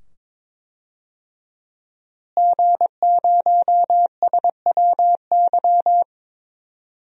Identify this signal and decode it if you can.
Morse code, heard as a keyed tone, about 22 wpm, 700 Hz. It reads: G0SWY
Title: DXCC-100 Call Signs - 5 Characters - Repeated 2x - 22wpm